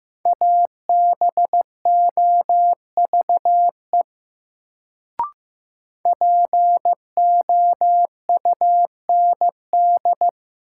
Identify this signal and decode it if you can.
Morse code, heard as a keyed tone, about 15 wpm, 700 Hz. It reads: ABOVE POUND
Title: Sentences from Top 1000 Words - Review Part 2 - New Word Drill - 15wpm